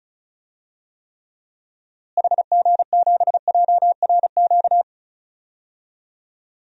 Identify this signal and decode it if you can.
Morse code, heard as a keyed tone, about 34 wpm, 700 Hz. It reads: HG7JRQ
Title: DXCC-100 Call Signs - 6 Characters - Repeated 2x - 35wpm